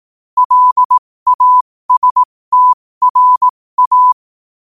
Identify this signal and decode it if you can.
Morse code, heard as a keyed tone, about 18 wpm, spaced as for 15 wpm, 1000 Hz. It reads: LASTRA